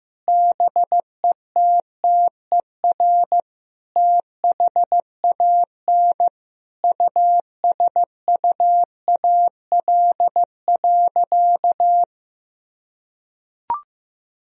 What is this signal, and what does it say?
Morse code, heard as a keyed tone, about 14 wpm, 700 Hz. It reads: BETTER THAN USUAL.
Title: Sentences from Top 1000 Words - Review Part 2 - 15wpm